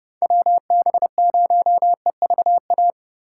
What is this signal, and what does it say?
Morse code, heard as a keyed tone, about 33 wpm, 700 Hz. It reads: WB0EVA